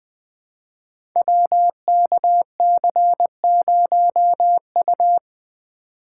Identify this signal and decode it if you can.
Morse code, heard as a keyed tone, about 20 wpm, 700 Hz. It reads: WKC0U
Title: Code Groups - 5 Letter:Numbers - 20wpm